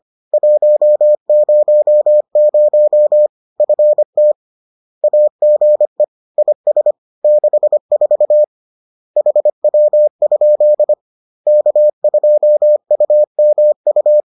Morse code, heard as a keyed tone, about 25 wpm, 600 Hz. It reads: 100 FT AGE IS 64 HW? K2UMU